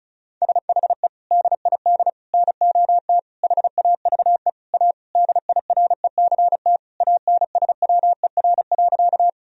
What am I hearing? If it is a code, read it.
Morse code, heard as a keyed tone, about 34 wpm, 700 Hz. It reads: SHE DID NOT HAVE A DIRECT ANSWER.
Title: Sentences from Top 1000 Words - Review Part 2 - 35wpm